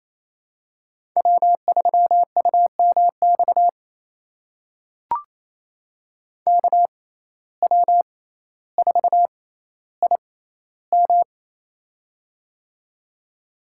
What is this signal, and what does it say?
Morse code, heard as a keyed tone, about 28 wpm, spaced as for 14 wpm, 700 Hz. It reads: W3UMX KW4IM
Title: USA Call Signs - 5 Characters - 6x Character Spacing - 28wpm